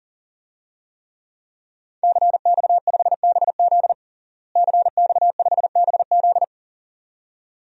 Morse code, heard as a keyed tone, about 37 wpm, 700 Hz. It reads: CX5BZ CX5BZ
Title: DXCC-100 Call Signs - 5 Characters - Repeated 2x - 40wpm